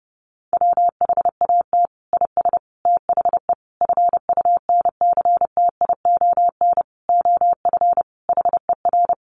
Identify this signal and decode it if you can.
Morse code, heard as a keyed tone, about 30 wpm, 700 Hz. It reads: WHAT IS THE FUNCTION OF HER
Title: Sentences from Top 1100 Words 30wpm